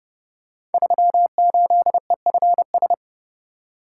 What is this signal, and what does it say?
Morse code, heard as a keyed tone, about 30 wpm, 700 Hz. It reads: 38EFS
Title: Code Groups - 5 Letter:Numbers - 30wpm